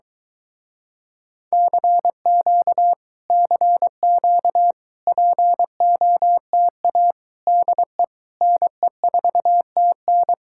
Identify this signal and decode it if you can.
Morse code, heard as a keyed tone, about 23 wpm, 700 Hz. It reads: CQ CQ POTA DE NE4TN